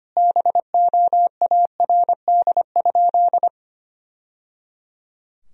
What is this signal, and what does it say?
Morse code, heard as a keyed tone, about 25 wpm, 700 Hz. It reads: BOARD?
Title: Sentences from Top 1000 Words - Review Part 4 - 25wpm